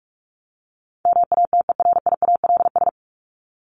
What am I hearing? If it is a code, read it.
Morse code, heard as a keyed tone, about 44 wpm, 700 Hz. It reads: MATERIALS
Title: ES:S Suffix - Top 100 Words with Base Words 45wpm